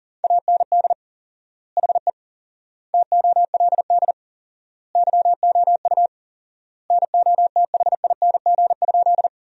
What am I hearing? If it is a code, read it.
Morse code, heard as a keyed tone, about 40 wpm, 700 Hz. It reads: AND HE TOLD YOU NOTHING?
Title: 4x Word-Spacing - Sentences from Top 500 Words - 40wpm